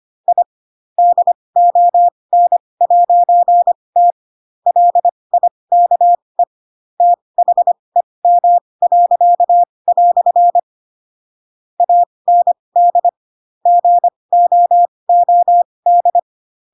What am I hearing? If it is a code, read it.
Morse code, heard as a keyed tone, about 25 wpm, spaced as for 20 wpm, 700 Hz. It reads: I DON'T LIKE THEM." AND GOOD
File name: Pinocchio / Pinocchio ch7 15 25 ik1hge